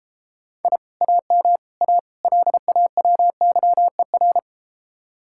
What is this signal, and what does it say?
Morse code, heard as a keyed tone, about 33 wpm, 700 Hz. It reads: I AM A LAWYER